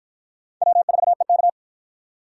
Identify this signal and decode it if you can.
Morse code, heard as a keyed tone, about 53 wpm, 700 Hz. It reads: W4EK